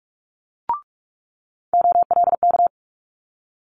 Morse code, heard as a keyed tone, about 46 wpm, 700 Hz. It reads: OLK
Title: Multi-letter Phonemes Quick Fire 45wpm